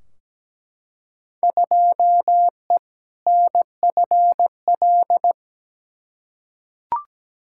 Morse code, heard as a keyed tone, about 17 wpm, 700 Hz. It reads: 2E NFL
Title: Field Day Exchanges Encore - 17wpm